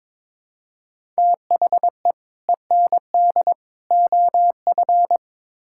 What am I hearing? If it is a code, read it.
Morse code, heard as a keyed tone, about 21 wpm, 700 Hz. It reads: THE END OF
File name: practice / 22wpm / Build A Phrase - 5-Word Phrases - 22wpm